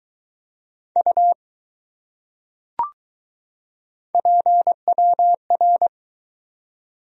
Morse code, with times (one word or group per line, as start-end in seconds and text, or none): 1.0-1.3 U
4.1-5.9 PWR